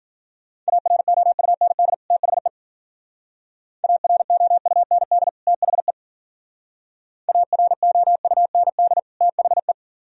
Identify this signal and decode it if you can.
Morse code, heard as a keyed tone, about 46 wpm, 700 Hz. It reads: AROUND THE AROUND THE AROUND THE
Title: Speed-Racing - Sets of 2 Words 40wpm